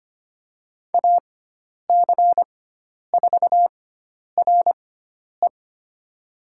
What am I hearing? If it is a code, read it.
Morse code, heard as a keyed tone, about 25 wpm, spaced as for 10 wpm, 700 Hz. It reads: AC4RE
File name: Lesson 16g - Letter c - Call Signs 25wpm